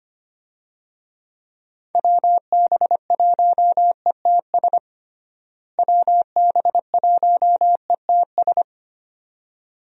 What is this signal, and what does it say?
Morse code, heard as a keyed tone, about 25 wpm, 700 Hz. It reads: WB1ETS WB1ETS
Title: USA Call Signs - 6 Characters - Repeated 2x - 25wpm